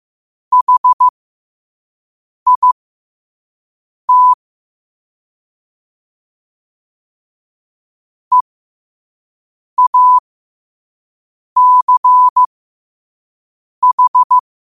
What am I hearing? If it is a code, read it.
Morse code, heard as a keyed tone, about 15 wpm, spaced as for 5 wpm, 1000 Hz. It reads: HIT EACH
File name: qotdfiles / 5wpm